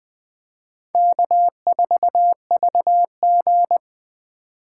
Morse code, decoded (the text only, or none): K4VG